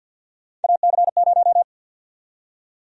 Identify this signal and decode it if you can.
Morse code, heard as a keyed tone, about 50 wpm, 700 Hz. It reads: AC0
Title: USA Call Sign Prefixes - Repeated 2x - 50wpm